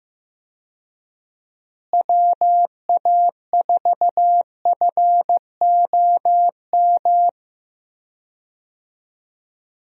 Morse code, heard as a keyed tone, about 15 wpm, 700 Hz. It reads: WA4FOM
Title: USA Call Signs - 6 Characters - Repeated 3x - 15wpm